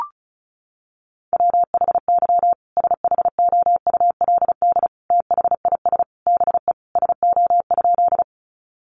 Morse code, heard as a keyed tone, about 34 wpm, 700 Hz. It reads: WHY SHOULD THIS BE SO?
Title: Sentences from Top 300 Words 35wpm